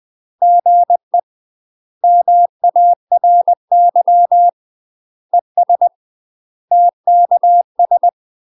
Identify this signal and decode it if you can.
Morse code, heard as a keyed tone, about 20 wpm, 700 Hz. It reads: GE MARY ES TKS
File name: UK MASTER QSOs TRAINING / 20WPM / UK MASTER QSOs 0016